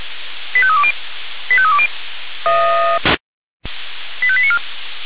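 Five-tone selective call (ZVEI), heard as quick runs of five tones, tones of about 70 ms. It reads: 85320 85320 85894
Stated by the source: ZVEI five-tone call received by SDR, from the Feuerwehr Orlamünde audio archive.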